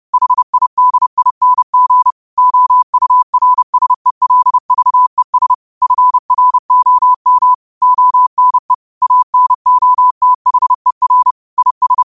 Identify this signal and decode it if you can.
Morse code, heard as a keyed tone, about 30 wpm, 1000 Hz. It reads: HIDING OURSELVES FROM ONE ANOTHER IS
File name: qotdfiles / 30wpm